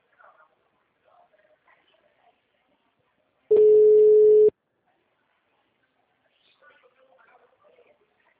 Ringback tone (Europe, 425 Hz).